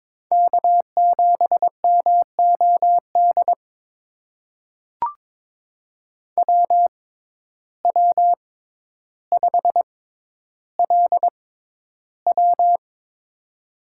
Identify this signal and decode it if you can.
Morse code, heard as a keyed tone, about 22 wpm, spaced as for 11 wpm, 700 Hz. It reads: K7MOD WW5LW
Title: USA Call Signs - 6x Character Spacing - 22wpm